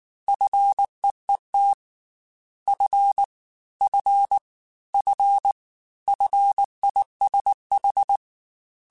Morse code, heard as a keyed tone, about 19 wpm, 800 Hz. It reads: FEET F F F FISH
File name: Fs / Single F mp3s / Single F Prefixes 3x2 Twice @18